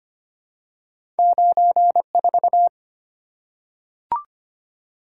Morse code, heard as a keyed tone, about 25 wpm, 700 Hz. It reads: 94
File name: Two Letter-Number Combinations - 25wpm